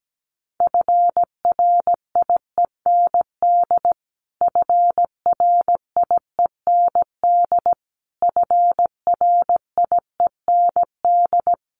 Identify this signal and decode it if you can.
Morse code, heard as a keyed tone, about 17 wpm, 700 Hz. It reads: FRIEND FRIEND FRIEND
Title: Top 500 Words Repeated 17wpm